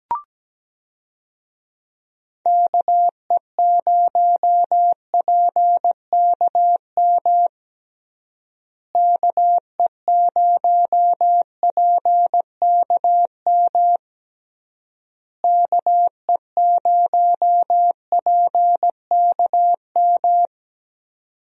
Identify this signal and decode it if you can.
Morse code, heard as a keyed tone, about 17 wpm, 700 Hz. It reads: KE0PKM KE0PKM KE0PKM
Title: USA Call Signs - 6 Characters - Repeated 3x - 17wpm